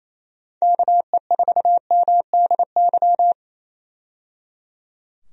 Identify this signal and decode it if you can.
Morse code, heard as a keyed tone, about 28 wpm, 700 Hz. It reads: KE4MDY
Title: USA Call Signs - 6 Characters - 28wpm